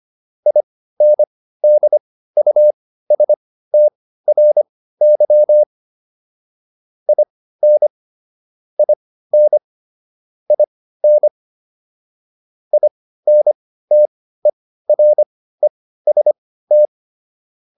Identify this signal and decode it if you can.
Morse code, heard as a keyed tone, about 25 wpm, spaced as for 15 wpm, 600 Hz. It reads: INDUSTRY IN IN IN INTEREST
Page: intermediate-practice-files